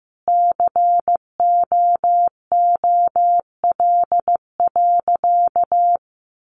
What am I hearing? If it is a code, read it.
Morse code, heard as a keyed tone, about 15 wpm, 700 Hz. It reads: COOL.
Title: Sentences from Top 1200 Words 15wpm